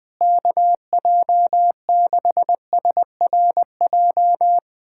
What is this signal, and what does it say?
Morse code, heard as a keyed tone, about 21 wpm, 700 Hz. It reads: KJ6SRJ
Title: USA Callsign DE Callsign 20wpm